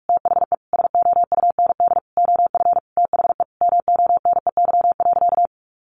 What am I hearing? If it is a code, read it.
Morse code, heard as a keyed tone, about 44 wpm, 700 Hz. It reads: THE SOUND OF THE MONEY.
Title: Sentences from Top 1100 Words 45wpm